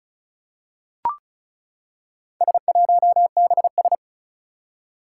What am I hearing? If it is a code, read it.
Morse code, heard as a keyed tone, about 36 wpm, 700 Hz. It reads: S1BS
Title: Four Challenging Letter-Number Combinations 35wpm